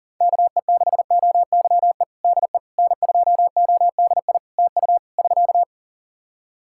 Morse code, heard as a keyed tone, about 37 wpm, 700 Hz. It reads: KE6OYE DE N2ODI TU <SK>
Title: K1USN SST QSO Practice 40wpm